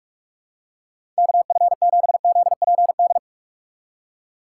Morse code, heard as a keyed tone, about 43 wpm, 700 Hz. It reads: KF7ZPD
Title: USA Call Signs - 6 Characters - 6x Character Spacing - 45wpm